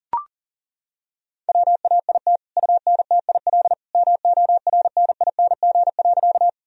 Morse code, heard as a keyed tone, about 40 wpm, 700 Hz. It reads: WAIT UNTIL MORNING.
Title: Sentences from Top 1000 Words - Review Part 1 - 40wpm